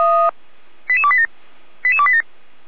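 Five-tone selective call (ZVEI), heard as quick runs of five tones, tones of about 80 ms. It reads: 88287 88287
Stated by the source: ZVEI five-tone selective call received on an SDR.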